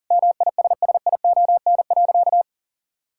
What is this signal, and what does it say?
Morse code, heard as a keyed tone, about 37 wpm, 700 Hz. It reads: MISSION.
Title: Sentences from Top 1600 Words 40wpm